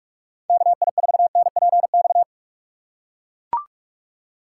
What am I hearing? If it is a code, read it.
Morse code, heard as a keyed tone, about 40 wpm, 700 Hz. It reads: KI4NPX